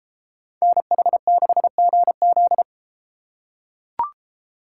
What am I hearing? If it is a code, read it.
Morse code, heard as a keyed tone, about 32 wpm, 700 Hz. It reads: NH6GZ